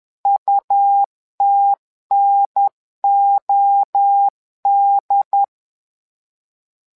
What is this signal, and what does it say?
Morse code, heard as a keyed tone, about 11 wpm, 800 Hz. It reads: UTNOD
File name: LEZIONI / 3 MWRO / 3 MWRO 10wpm 2148